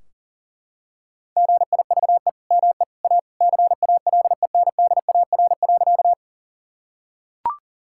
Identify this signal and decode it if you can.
Morse code, heard as a keyed tone, about 40 wpm, 700 Hz. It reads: GIVE ME A CALENDAR.
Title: Sentences from Top 1600 Words 40wpm